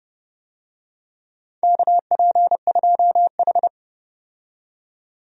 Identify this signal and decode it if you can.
Morse code, heard as a keyed tone, about 32 wpm, 700 Hz. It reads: KP2H
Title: DXCC-100 Call Signs - 4 Characters - 30wpm